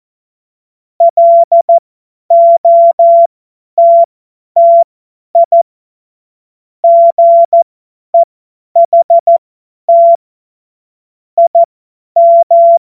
Morse code, heard as a keyed tone, about 14 wpm, spaced as for 10 wpm, 680 Hz. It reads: LOTTI GEHT IM
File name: Morsekurs / Witze / Witz 21.November 2023 50 F70